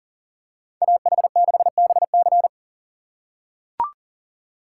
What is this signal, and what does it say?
Morse code, heard as a keyed tone, about 40 wpm, 700 Hz. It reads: AH6BC
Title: USA Call Signs - 5 Characters - 3x Character Spacing - 40wpm